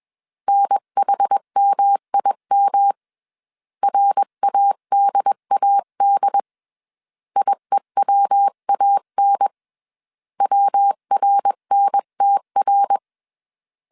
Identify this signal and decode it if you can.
Morse code, heard as a keyed tone, about 21 wpm, 800 Hz. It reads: NHMIM RADAD IEWAN WRNTR
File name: CW / DET / 1 3 1500C S B 20wpm 1754